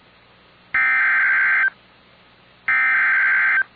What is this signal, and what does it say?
Emergency Alert System (SAME) tones (header bursts), heard as harsh buzzing bursts. It is heard from 0.7 to 3.6 s.